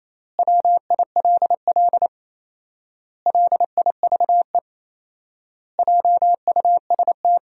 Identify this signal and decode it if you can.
Morse code, heard as a keyed tone, about 28 wpm, 700 Hz. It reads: WILL LIVE JUST
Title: 4x Word-Spacing - Sentences from Top 300 Words - 28wpm